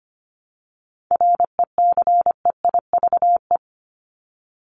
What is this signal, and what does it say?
Morse code, heard as a keyed tone, about 25 wpm, 700 Hz. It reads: RECEIVE
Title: Top 751-1000 Words 25wpm